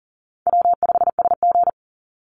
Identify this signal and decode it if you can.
Morse code, heard as a keyed tone, about 40 wpm, 700 Hz. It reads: W5SG
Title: Easy Call Signs Repeated 2x - 40wpm